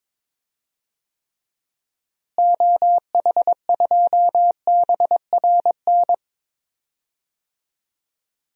Morse code, heard as a keyed tone, about 22 wpm, 700 Hz. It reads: OH2BRN
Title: DXCC-100 Call Signs - 6 Characters - Repeated 3x - 22wpm